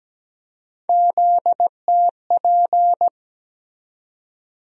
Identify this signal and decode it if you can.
Morse code, heard as a keyed tone, about 17 wpm, 700 Hz. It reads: ZTP
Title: USA Call Sign Postfixes - 3 Letters 17wpm